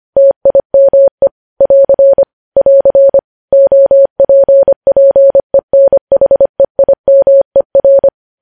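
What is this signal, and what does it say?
Morse code, heard as a keyed tone, about 25 wpm, 550 Hz. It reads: TIME <AR> <AR> OPPENHEIMER
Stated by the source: Morse code practice of news headlines at 25 wpm, opening with vvv.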